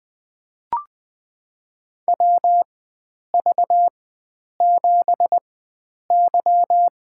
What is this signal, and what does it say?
Morse code, heard as a keyed tone, about 20 wpm, spaced as for 9 wpm, 700 Hz. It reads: WV7Y
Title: USA Call Signs - 4 Characters - 4x Character Spacing - 20wpm